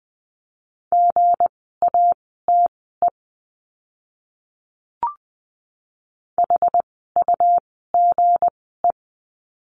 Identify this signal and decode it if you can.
Morse code, heard as a keyed tone, about 20 wpm, spaced as for 14 wpm, 700 Hz. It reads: GATE HUGE
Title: 160 Lesson 26f - Letter g - Words 20wpm